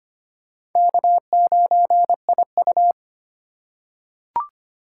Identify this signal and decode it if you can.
Morse code, heard as a keyed tone, about 25 wpm, 700 Hz. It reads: K9IU
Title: USA Call Signs - 4 Characters - 3x Character Spacing - 25wpm